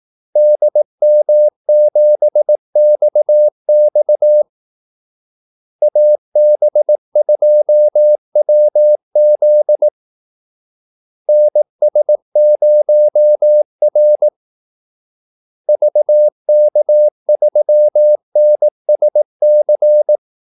Morse code, heard as a keyed tone, about 18 wpm, 600 Hz. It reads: DM7XX AB2WZ NS0R VK3NSC
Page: Kaldesignaler 18 wpm